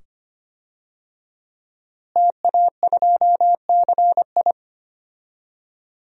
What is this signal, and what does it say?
Morse code, heard as a keyed tone, about 25 wpm, 700 Hz. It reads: TA2CI
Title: DXCC-100 Call Signs - 5 Characters - Repeated 2x - 25wpm